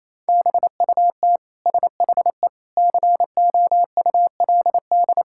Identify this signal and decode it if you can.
Morse code, heard as a keyed tone, about 27 wpm, 700 Hz. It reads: BUT SHE COULD
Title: Sentences from Top 1000 Words - Review Part 6 - 28wpm